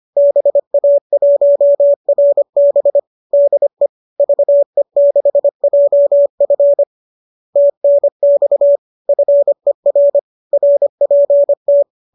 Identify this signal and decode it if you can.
Morse code, heard as a keyed tone, about 25 wpm, 590 Hz. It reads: BA1RB DE VE6JF TNX FER RPT
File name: qso203 25